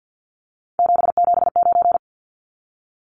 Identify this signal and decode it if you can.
Morse code, heard as a keyed tone, about 50 wpm, 700 Hz. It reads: 779
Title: Numbers - 3-Digit 50wpm